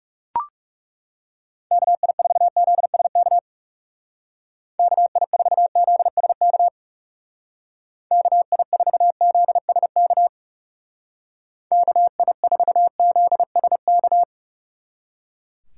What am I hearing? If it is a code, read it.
Morse code, heard as a keyed tone, about 37 wpm, 700 Hz. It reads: KI4ZSK KI4ZSK KI4ZSK KI4ZSK
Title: Speed Racing - USA 2x3 Call Signs 30wpm